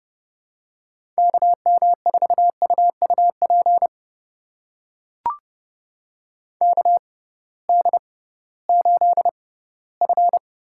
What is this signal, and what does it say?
Morse code, heard as a keyed tone, about 30 wpm, spaced as for 18 wpm, 700 Hz. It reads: KM4UUP KD8F
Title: USA Call Signs - 6 Characters - 6x Character Spacing - 30wpm